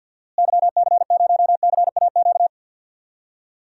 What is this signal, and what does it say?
Morse code, heard as a keyed tone, about 46 wpm, 700 Hz. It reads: YC0CAQ
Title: DXCC-100 Call Signs - 6 Characters - Repeated 2x - 50wpm